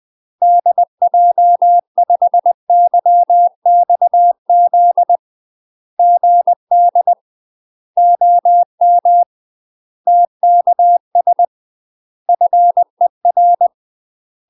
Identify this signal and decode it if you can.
Morse code, heard as a keyed tone, about 20 wpm, 700 Hz. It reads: DJ5YXZ GD OM TKS FER
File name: UK MASTER QSOs TRAINING / 20WPM / UK MASTER QSOs 0039